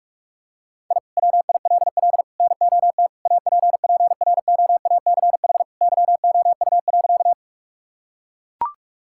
Morse code, heard as a keyed tone, about 41 wpm, 700 Hz. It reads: I WILL NOT APPROACH YOU.
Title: Sentences from Top 1000 Words - Review Part 6 - 45wpm